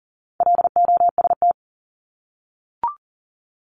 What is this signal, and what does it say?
Morse code, heard as a keyed tone, about 40 wpm, 700 Hz. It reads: LOST
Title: Top 1251-1500 Words 40wpm